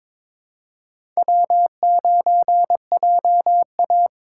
Morse code, heard as a keyed tone, about 22 wpm, 700 Hz. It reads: W9JA
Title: USA Call Signs - 4 Characters - 4x Character Spacing - 22wpm